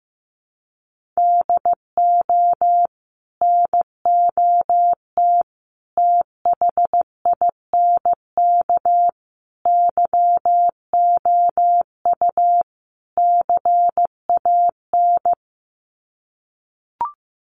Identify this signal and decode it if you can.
Morse code, heard as a keyed tone, about 15 wpm, 700 Hz. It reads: DO NOT THINK YOU CAN
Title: Sets of 5 Words 15wpm